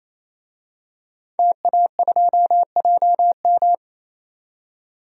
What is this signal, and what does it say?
Morse code, heard as a keyed tone, about 28 wpm, 700 Hz. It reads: TA2JM
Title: DXCC-100 Call Signs - 5 Characters - 28wpm